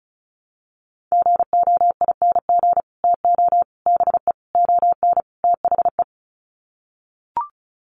Morse code, heard as a keyed tone, about 34 wpm, 700 Hz. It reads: GOING TO BE ON THE
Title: Sets of 5 Words Encore2 35wpm